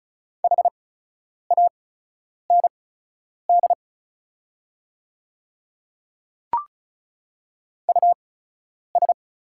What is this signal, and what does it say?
Morse code, heard as a keyed tone, about 36 wpm, spaced as for 10 wpm, 700 Hz. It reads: HAND US
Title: 8x Character Spacing - Top 300 Words - 35wpm